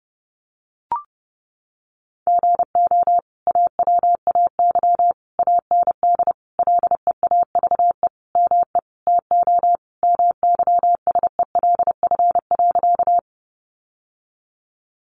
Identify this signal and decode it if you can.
Morse code, heard as a keyed tone, about 30 wpm, 700 Hz. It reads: GO AWAY AND LEAVE ME TO MYSELF.
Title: Sentences from Top 1300 Words 30wpm